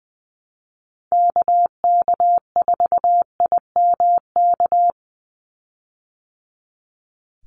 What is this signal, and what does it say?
Morse code, heard as a keyed tone, about 20 wpm, 700 Hz. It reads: KK4IMK
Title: USA Call Signs 20wpm